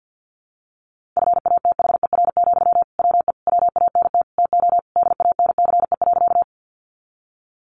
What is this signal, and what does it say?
Morse code, heard as a keyed tone, about 50 wpm, 700 Hz. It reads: FATHER, WE WANT TO DANCE.
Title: Sentences from Top 700 Words 50wpm